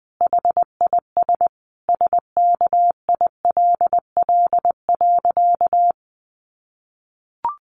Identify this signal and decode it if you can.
Morse code, heard as a keyed tone, about 20 wpm, 700 Hz. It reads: HIS SKILL.
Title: Sentences from Top 1200 Words 20wpm